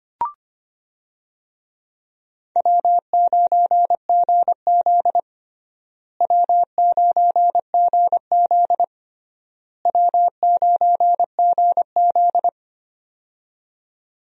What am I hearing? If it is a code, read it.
Morse code, heard as a keyed tone, about 25 wpm, 700 Hz. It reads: W9GZ W9GZ W9GZ
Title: USA Call Signs - 4 Characters - Repeated 3x - 25wpm